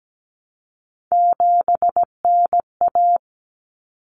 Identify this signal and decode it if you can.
Morse code, heard as a keyed tone, about 17 wpm, 700 Hz. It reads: 7NA
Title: Common Cut Numbers - 3-Digit 17wpm